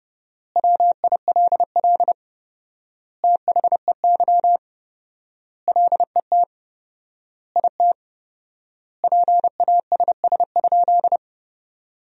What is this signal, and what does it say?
Morse code, heard as a keyed tone, about 31 wpm, 700 Hz. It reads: WILL THEY LET IT PASS?
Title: 4x Word-Spacing - Sentences from Top 400 Words - 30wpm